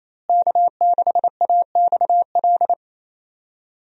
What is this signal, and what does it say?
Morse code, heard as a keyed tone, about 28 wpm, 700 Hz. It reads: K6AXL
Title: USA Call Signs - 5 Characters - Repeated 3x - 28wpm